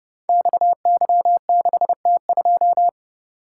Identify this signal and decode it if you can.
Morse code, heard as a keyed tone, about 30 wpm, 700 Hz. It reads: XY6T2
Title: Code Groups - 5 Letter:Numbers - 30wpm